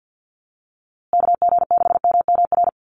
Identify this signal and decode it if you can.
Morse code, heard as a keyed tone, about 50 wpm, 700 Hz. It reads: KG6MMR
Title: USA Call Signs Repeated 3x 50wpm